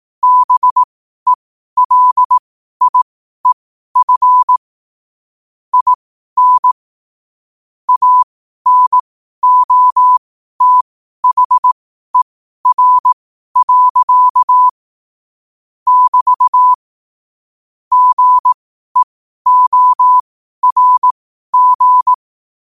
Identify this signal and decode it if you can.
Morse code, heard as a keyed tone, about 18 wpm, spaced as for 12 wpm, 1000 Hz. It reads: BELIEF IN ANOTHER. <BT> GEORG